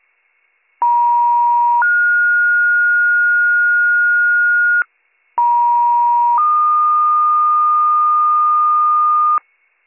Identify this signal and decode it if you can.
Two-tone page, heard as a one-second tone, then a three-second tone, about 950 then 1430 Hz, about 950 then 1220 Hz, with noise. It is heard from 0.8 to 9.4 s.